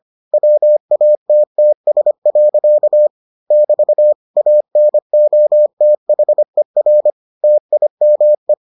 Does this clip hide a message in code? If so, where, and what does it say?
Yes: Morse code from 0.3 to 8.5 s, heard as a keyed tone, about 25 wpm, 600 Hz. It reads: WATTS. <BT> ANOTHER TIME